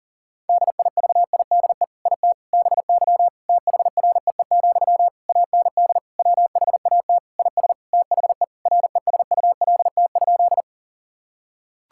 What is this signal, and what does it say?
Morse code, heard as a keyed tone, about 37 wpm, 700 Hz. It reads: DIVIDE IT BY THREE, AND WHAT IS THE RESULT?
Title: Sentences from Top 1600 Words 40wpm